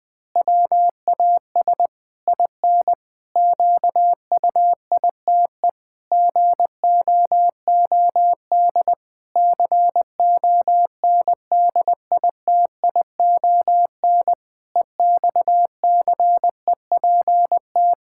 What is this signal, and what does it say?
Morse code, heard as a keyed tone, about 20 wpm, 700 Hz. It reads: WAS IN QUITE GOOD CONDITION EXCEPT
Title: Pangram Challenge v2 - 20wpm